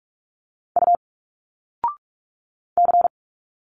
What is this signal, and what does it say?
Morse code, heard as a keyed tone, about 46 wpm, 700 Hz. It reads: U C
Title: Single Letters 45wpm